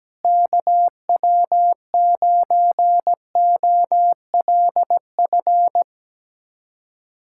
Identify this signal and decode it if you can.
Morse code, heard as a keyed tone, about 17 wpm, 700 Hz. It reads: KW9OLF
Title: USA Callsign DE Callsign 17wpm